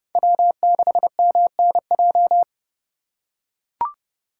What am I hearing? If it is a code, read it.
Morse code, heard as a keyed tone, about 28 wpm, 700 Hz. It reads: W6MNJ